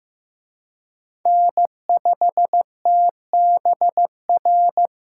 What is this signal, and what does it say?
Morse code, heard as a keyed tone, about 18 wpm, 700 Hz. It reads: N5TBR
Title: Speed Racing - USA 1x3 Call Signs 15wpm